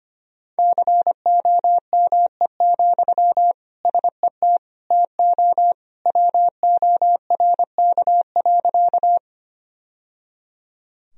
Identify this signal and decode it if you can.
Morse code, heard as a keyed tone, about 25 wpm, 700 Hz. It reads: COME, SET TO WORK.
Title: Sentences from Top 1000 Words - Review Part 1 - 25wpm